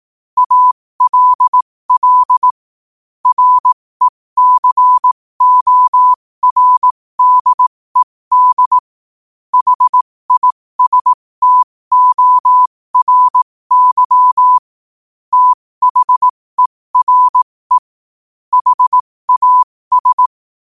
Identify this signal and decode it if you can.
Morse code, heard as a keyed tone, about 18 wpm, spaced as for 15 wpm, 1000 Hz. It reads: ALL RECORDED HISTORY THERE HAS
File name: qotdfiles / 15wpm